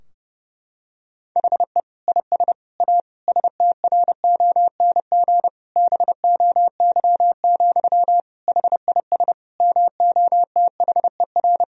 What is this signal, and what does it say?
Morse code, heard as a keyed tone, about 32 wpm, 700 Hz. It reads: HE IS A STRONG BOY, HIS MOTHER
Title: Sentences from Top 1000 Words - Review Part 3 - 30wpm